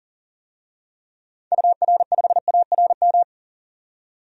Morse code, heard as a keyed tone, about 40 wpm, 700 Hz. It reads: UR5ARM
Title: DXCC-100 Call Signs - 6 Characters - 40wpm